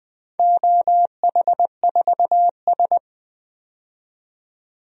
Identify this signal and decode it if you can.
Morse code, heard as a keyed tone, about 20 wpm, 700 Hz. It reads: OH4S
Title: DXCC-100 Call Signs - 4 Characters - 20wpm